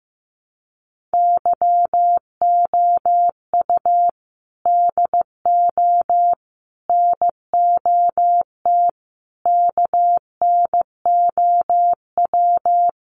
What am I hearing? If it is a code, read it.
Morse code, heard as a keyed tone, about 15 wpm, 700 Hz. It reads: YOU DO NOT KNOW
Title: Sets of 5 Words Encore 15wpm